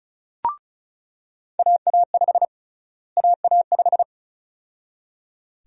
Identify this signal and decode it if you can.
Morse code, heard as a keyed tone, about 34 wpm, 700 Hz. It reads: AA5 AA5
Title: USA Call Sign Prefixes - Repeated 2x - 35wpm